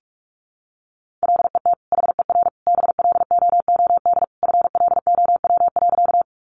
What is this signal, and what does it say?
Morse code, heard as a keyed tone, about 44 wpm, 700 Hz. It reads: LET HER BLOOD FLOW.
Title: Sentences from Top 1200 Words 45wpm